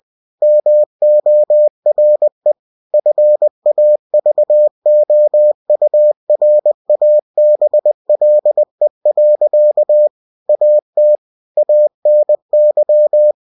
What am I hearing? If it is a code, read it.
Morse code, heard as a keyed tone, about 20 wpm, 600 Hz. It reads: MORE FAVOURABLE. AT ANY
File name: cw-books / winnie-the-pooh / winnie-20wpm-page-0097